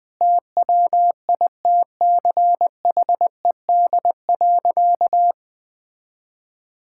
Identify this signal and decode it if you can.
Morse code, heard as a keyed tone, about 20 wpm, 700 Hz. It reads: TWITCHED.
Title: Sentences with 2-Word Phrases - Spoken Beginning - 20wpm